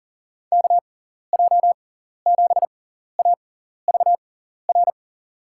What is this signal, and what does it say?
Morse code, heard as a keyed tone, about 40 wpm, spaced as for 14 wpm, 700 Hz. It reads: KJ7AVR
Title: USA Call Signs - 6x Character Spacing - 40wpm